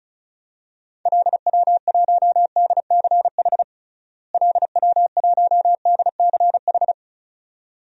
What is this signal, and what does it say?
Morse code, heard as a keyed tone, about 36 wpm, 700 Hz. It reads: LW1DCH LW1DCH
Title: DXCC-100 Call Signs - 6 Characters - Repeated 2x - 35wpm